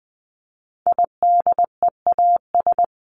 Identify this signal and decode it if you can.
Morse code, heard as a keyed tone, about 20 wpm, 700 Hz. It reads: IDEAS
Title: ES:S Suffix - Top 100 Words with Base Words 20wpm